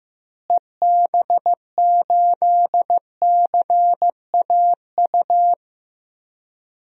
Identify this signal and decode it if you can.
Morse code, heard as a keyed tone, about 15 wpm, 700 Hz. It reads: EB8CAU
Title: DXCC-100 Call Signs - 6 Characters - Repeated 3x - 15wpm